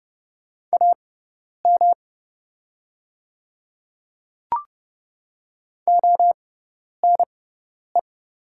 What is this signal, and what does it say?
Morse code, heard as a keyed tone, about 31 wpm, spaced as for 10 wpm, 700 Hz. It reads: AM ONE
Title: 6x Character Spacing - Top 500 Words - 30wpm